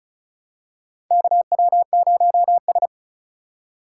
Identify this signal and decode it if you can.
Morse code, heard as a keyed tone, about 36 wpm, 700 Hz. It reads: KW0S
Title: USA Call Signs - 4x Character Spacing - 35wpm